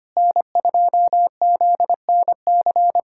Morse code, heard as a keyed tone, about 25 wpm, 700 Hz. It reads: N2ZNC
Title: USA Callsign DE Callsign 25wpm